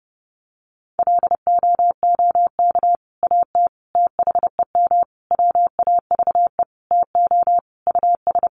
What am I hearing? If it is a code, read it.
Morse code, heard as a keyed tone, about 30 wpm, 700 Hz. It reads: LOOK AT THEM WAVE TO US
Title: Sentences from Top 1400 Words 30wpm